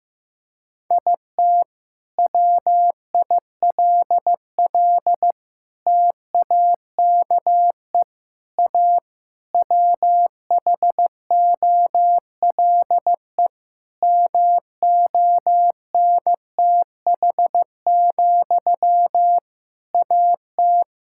Morse code, heard as a keyed tone, about 15 wpm, 700 Hz. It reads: IT WILL TAKE A WHOLE MONTH, AT
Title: Sentences from Top 1000 Words - Review Part 5 - 15wpm